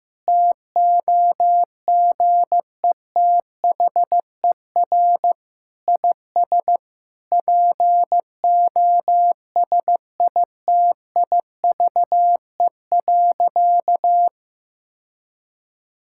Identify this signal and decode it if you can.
Morse code, heard as a keyed tone, about 15 wpm, 700 Hz. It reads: TOGETHER IS POSITIVE.
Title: Sentences from Top 1000 Words - Review Part 5 - 15wpm